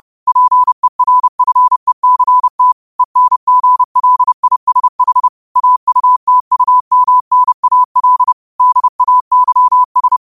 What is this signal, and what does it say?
Morse code, heard as a keyed tone, about 30 wpm, 1000 Hz. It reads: PERFECT ENGLISH AUTUMNAL DAYS
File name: qotdfiles / 30wpm